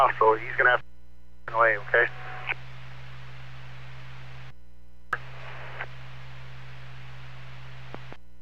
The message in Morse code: none

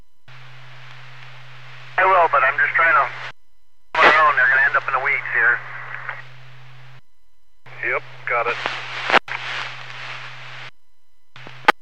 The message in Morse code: none

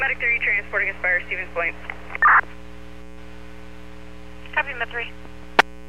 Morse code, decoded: none

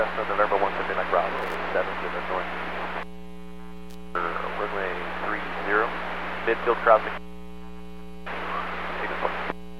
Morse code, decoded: none

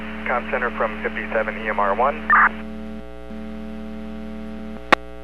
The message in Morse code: none

